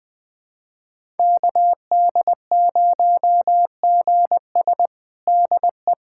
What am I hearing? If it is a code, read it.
Morse code, heard as a keyed tone, about 20 wpm, 700 Hz. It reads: KD0GS DE
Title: USA Callsign DE Callsign - 5 Characters - 20wpm